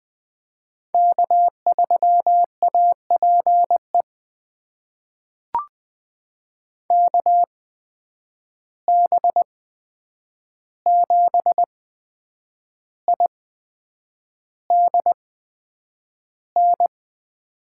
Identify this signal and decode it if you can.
Morse code, heard as a keyed tone, about 20 wpm, spaced as for 8 wpm, 700 Hz. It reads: K3APE KB7IDN
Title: USA Call Signs - 8x Character Spacing - 20wpm